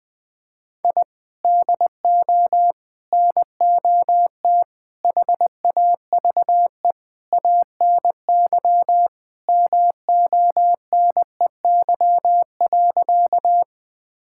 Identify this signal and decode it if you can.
Morse code, heard as a keyed tone, about 20 wpm, 700 Hz. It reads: I DO NOT HAVE ANY MONEY.
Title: Sentences with 5-Word Phrases - 20wpm